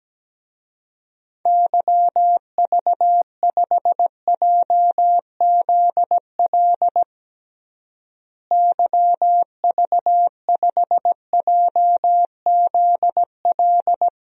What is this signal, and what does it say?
Morse code, heard as a keyed tone, about 17 wpm, 700 Hz. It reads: YV5JZL YV5JZL
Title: DXCC-100 Call Signs - 6 Characters - Repeated 2x - 17wpm